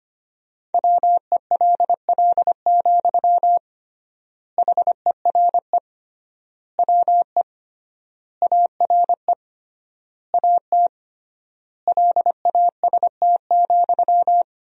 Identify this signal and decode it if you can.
Morse code, heard as a keyed tone, about 25 wpm, 700 Hz. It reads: WELL, HERE WE ARE AT LAST,
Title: 3x Word-Spacing - Sentences from Top 300 Words - 25wpm